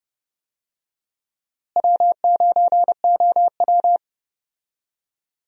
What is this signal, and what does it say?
Morse code, heard as a keyed tone, about 32 wpm, 700 Hz. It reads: W9OW
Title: USA Call Signs - 4 Characters - Repeated 2x - 30wpm